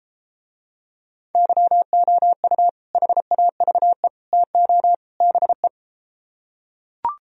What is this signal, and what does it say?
Morse code, heard as a keyed tone, about 33 wpm, 700 Hz. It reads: YOU HAVE TO BE